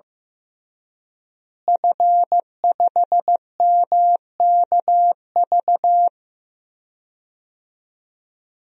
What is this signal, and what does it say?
Morse code, heard as a keyed tone, about 15 wpm, 700 Hz. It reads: F5MKV
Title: DXCC-100 Call Signs - 5 Characters - Repeated 3x - 15wpm